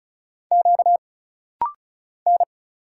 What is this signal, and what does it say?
Morse code, heard as a keyed tone, about 33 wpm, 700 Hz. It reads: Q N
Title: Single Letters - Rapid-Fire 35wpm